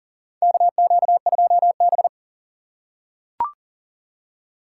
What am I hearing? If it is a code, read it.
Morse code, heard as a keyed tone, about 40 wpm, 700 Hz. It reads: KQ2B
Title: USA Call Signs - 4 Characters - 2x Character Spacing - 40wpm